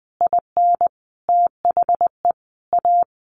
Morse code, heard as a keyed tone, about 20 wpm, 700 Hz. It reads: IN THE A
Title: Additional Easy Phrases 20wpm